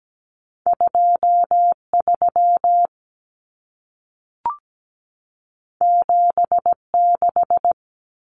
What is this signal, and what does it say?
Morse code, heard as a keyed tone, about 17 wpm, 700 Hz. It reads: 23 76
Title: Numbers - 2-Digit 17wpm